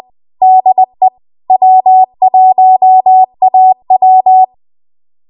Morse code, heard as a keyed tone, about 20 wpm, 750 Hz. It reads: DE W1AW